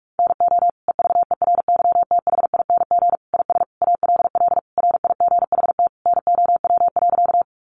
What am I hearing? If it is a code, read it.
Morse code, heard as a keyed tone, about 44 wpm, 700 Hz. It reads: NO EVERYTHING IS ALL RIGHT NOW.
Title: Sentences from Top 700 Words 45wpm